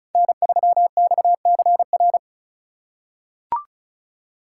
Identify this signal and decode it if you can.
Morse code, heard as a keyed tone, about 34 wpm, 700 Hz. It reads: N3XCR
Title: USA Call Signs - 5 Characters - 2x Character Spacing - 35wpm